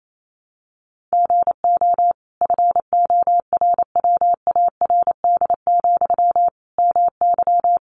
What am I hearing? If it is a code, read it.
Morse code, heard as a keyed tone, about 28 wpm, 700 Hz. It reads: GO FORWARD, MY
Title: Sentences from Top 1300 Words 28wpm